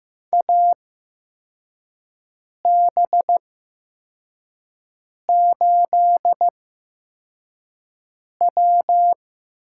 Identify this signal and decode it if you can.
Morse code, heard as a keyed tone, about 15 wpm, spaced as for 4 wpm, 700 Hz. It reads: AB8W
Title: USA Call Signs - 8x Character Spacing - 15wpm